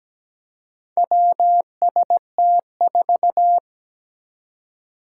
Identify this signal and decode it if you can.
Morse code, heard as a keyed tone, about 17 wpm, 700 Hz. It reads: WST4
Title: Four Challenging Letter-Number Combinations 17wpm